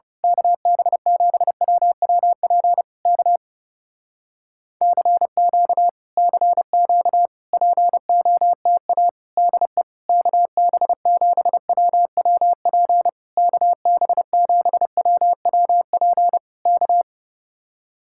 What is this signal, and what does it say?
Morse code, heard as a keyed tone, about 28 wpm, 700 Hz. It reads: KB7WWP K CQ CQ POTA DE KB7WWP KB7WWP K
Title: Speed-Racing Common QSO Phrases Encore - 30wpm